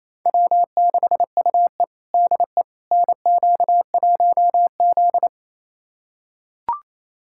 Morse code, heard as a keyed tone, about 28 wpm, 700 Hz. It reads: W6UE DE NQ1Z
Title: USA Callsign DE Callsign - 4 Characters - 28wpm